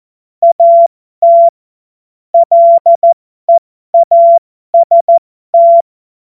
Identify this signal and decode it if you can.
Morse code, heard as a keyed tone, about 14 wpm, spaced as for 12 wpm, 680 Hz. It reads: AT LEAST